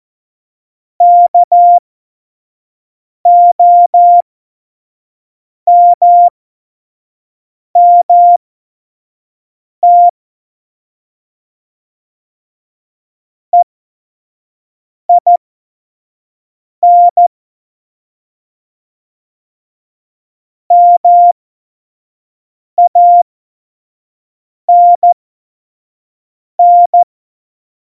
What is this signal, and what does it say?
Morse code, heard as a keyed tone, about 14 wpm, spaced as for 5 wpm, 700 Hz. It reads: KOMMT EIN MANN
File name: Morsekurs / Witze / Witz 06 March 2018, 25 F70